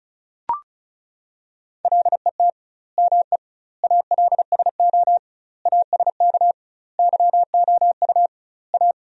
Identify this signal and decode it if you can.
Morse code, heard as a keyed tone, about 36 wpm, 700 Hz. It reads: LET ME ALSO ASK YOU A
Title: 2x Word-Spacing - Sentences from Top 400 Words - 35wpm